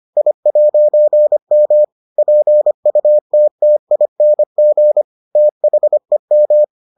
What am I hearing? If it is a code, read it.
Morse code, heard as a keyed tone, about 25 wpm, 600 Hz. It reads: I'M PUTTING THEM